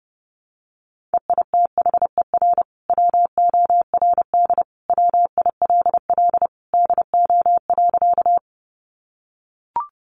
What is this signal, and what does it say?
Morse code, heard as a keyed tone, about 30 wpm, 700 Hz. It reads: EITHER WORD WILL DO.
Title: Sentences from Top 600 Words 30wpm